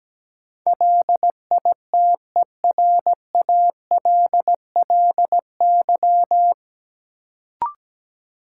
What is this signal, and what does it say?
Morse code, heard as a keyed tone, about 17 wpm, 700 Hz. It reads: LITERALLY
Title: Sentences from Top 1400 Words - New Word Drill - 17wpm